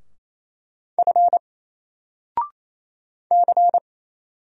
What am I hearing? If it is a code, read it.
Morse code, heard as a keyed tone, about 28 wpm, 700 Hz. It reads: F C
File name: Single-Letter-Number-28wpm